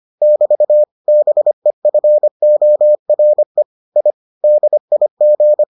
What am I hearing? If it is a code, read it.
Morse code, heard as a keyed tone, about 25 wpm, 600 Hz. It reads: <BT> BEFORE I DIG